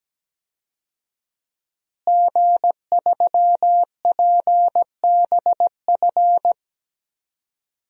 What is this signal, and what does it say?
Morse code, heard as a keyed tone, about 17 wpm, 700 Hz. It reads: G3PBF
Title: DXCC-100 Call Signs - 5 Characters - Repeated 3x - 17wpm